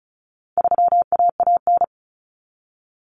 Morse code, heard as a keyed tone, about 34 wpm, 700 Hz. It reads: 3AAN
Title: Common Cut Numbers - 4-Digit 35wpm